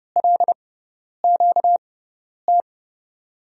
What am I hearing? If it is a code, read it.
Morse code, heard as a keyed tone, about 31 wpm, spaced as for 10 wpm, 700 Hz. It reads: LQT
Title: USA Call Signs - 5 Characters - 6x Character Spacing - 30wpm